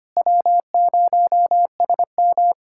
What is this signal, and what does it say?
Morse code, heard as a keyed tone, about 25 wpm, 700 Hz. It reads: W0SM
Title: USA Callsign DE Callsign - 4 Characters - 25wpm